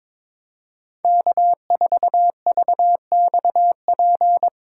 Morse code, heard as a keyed tone, about 22 wpm, 700 Hz. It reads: K4VXP
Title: USA Call Signs - 3x Character Spacing - 22wpm